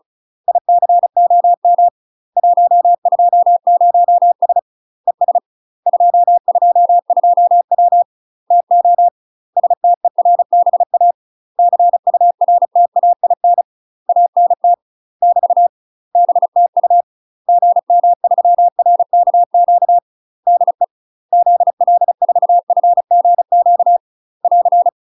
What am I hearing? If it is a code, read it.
Morse code, heard as a keyed tone, about 34 wpm, 700 Hz. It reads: ICOM 120S ES 222W TO STERBA CURTAIN ANT <BT> BTU GM3RKQ DE ZL4FGQ <AR>
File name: UK MASTER QSOs TRAINING / 35WPM / UK MASTER QSOs 0000